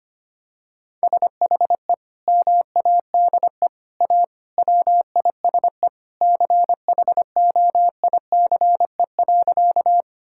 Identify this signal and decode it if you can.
Morse code, heard as a keyed tone, about 25 wpm, 700 Hz. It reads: SHE MADE A WISE CHOICE.
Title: Sentences from Top 1600 Words 25wpm